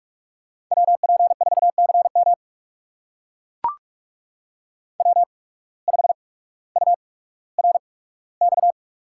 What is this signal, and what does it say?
Morse code, heard as a keyed tone, about 42 wpm, spaced as for 19 wpm, 700 Hz. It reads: WP4CM W5URX
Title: USA Call Signs - 8x Character Spacing - 45wpm